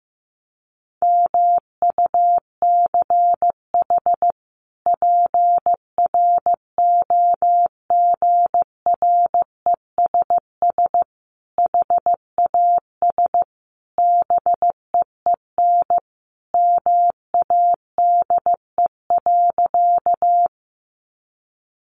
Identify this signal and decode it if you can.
Morse code, heard as a keyed tone, about 15 wpm, 700 Hz. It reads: MUCH PROGRESS HAS BEEN MADE.
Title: Sentences from Top 1400 Words 15wpm